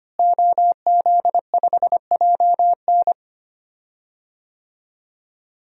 Morse code, heard as a keyed tone, about 25 wpm, 700 Hz. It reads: OZ5JN
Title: DXCC-100 Call Signs - 5 Characters - Repeated 3x - 25wpm